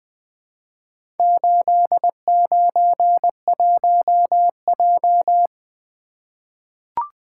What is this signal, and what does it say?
Morse code, heard as a keyed tone, about 20 wpm, 700 Hz. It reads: 891J
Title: Four Letter-Number Combinations - 20wpm